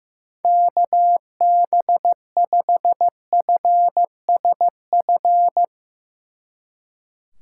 Morse code, heard as a keyed tone, about 16 wpm, 700 Hz. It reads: KB5FSF
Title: Speed Racing - USA 2x3 Call Signs 15wpm